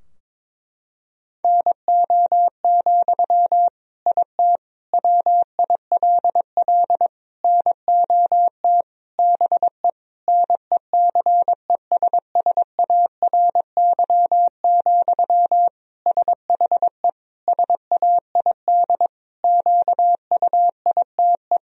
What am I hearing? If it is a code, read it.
Morse code, heard as a keyed tone, about 22 wpm, 700 Hz. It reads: NO, IT WILL NOT BE NECESSARY, SHE SAID QUITE
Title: Sentences from Top 1000 Words - Review Part 4 - 22wpm